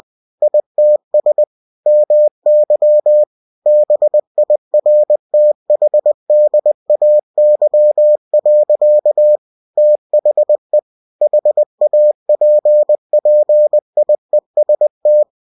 Morse code, heard as a keyed tone, about 20 wpm, 600 Hz. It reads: ITS MY BIRTHDAY. THE HAPPIEST
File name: cw-books / winnie-the-pooh / winnie-20wpm-page-0046